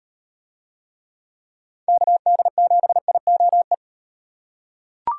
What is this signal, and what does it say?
Morse code, heard as a keyed tone, about 38 wpm, 700 Hz. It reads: KD7IOE